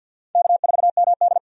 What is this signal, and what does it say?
Morse code, heard as a keyed tone, about 50 wpm, 700 Hz. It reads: K4MD
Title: USA Call Signs - 4 Characters - 50wpm